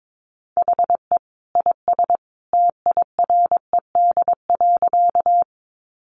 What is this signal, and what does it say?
Morse code, heard as a keyed tone, about 22 wpm, 700 Hz. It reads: HE IS TIRED.
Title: Additional Easy Phrases 22wpm